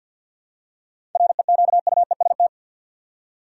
Morse code, heard as a keyed tone, about 47 wpm, 700 Hz. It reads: REQUEST
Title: Seven Letter Words - 50wpm